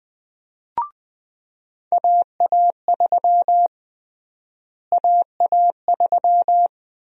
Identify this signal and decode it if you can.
Morse code, heard as a keyed tone, about 20 wpm, 700 Hz. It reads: AA3 AA3
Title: USA Call Sign Prefixes - Repeated 2x - 20wpm